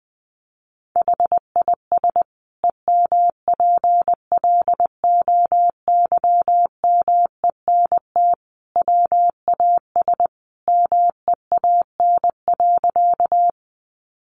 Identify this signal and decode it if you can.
Morse code, heard as a keyed tone, about 20 wpm, 700 Hz. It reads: HIS EMPLOYMENT WAS MEAN.
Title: Sentences from Top 1500 Words 20wpm